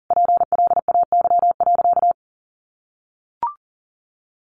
Morse code, heard as a keyed tone, about 40 wpm, 700 Hz. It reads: PLAY.
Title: Sentences from Top 900 Words 40wpm